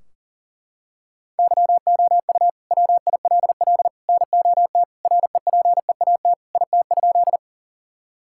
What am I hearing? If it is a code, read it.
Morse code, heard as a keyed tone, about 40 wpm, 700 Hz. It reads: YOU WILL NOT REPEAT IT?
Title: Sentences from Top 1600 Words 40wpm